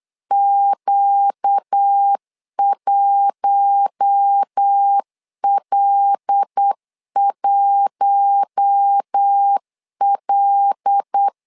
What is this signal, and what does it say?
Morse code, heard as a keyed tone, about 8 wpm, 800 Hz. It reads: Q1L1L
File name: LEZIONI / 5 LBVC / 5 L 8wpm 2831